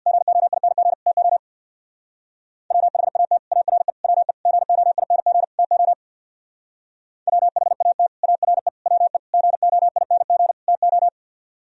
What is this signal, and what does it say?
Morse code, heard as a keyed tone, about 64 wpm, 700 Hz. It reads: GOING TO WHAT ARE WE GOING TO WHAT ARE WE GOING TO